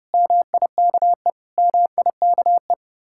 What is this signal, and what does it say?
Morse code, heard as a keyed tone, about 29 wpm, 700 Hz. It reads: MIKE MIKE